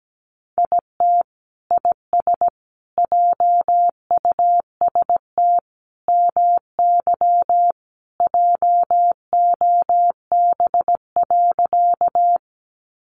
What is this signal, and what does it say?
Morse code, heard as a keyed tone, about 17 wpm, 700 Hz. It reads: IT IS JUST MY JOB.
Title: Sentences from Top 600 Words 17wpm